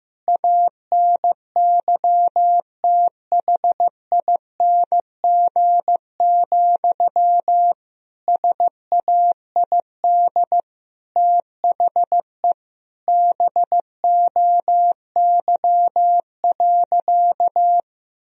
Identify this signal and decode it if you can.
Morse code, heard as a keyed tone, about 15 wpm, 700 Hz. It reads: ANYTHING, SAID THE BOY.
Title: Sentences from Top 1000 Words - Review Part 4 - 15wpm